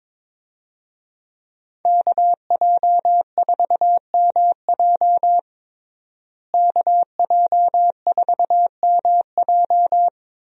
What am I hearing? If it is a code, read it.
Morse code, heard as a keyed tone, about 22 wpm, 700 Hz. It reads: KJ4MJ KJ4MJ
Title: USA Call Signs - 5 Characters - Repeated 2x - 22wpm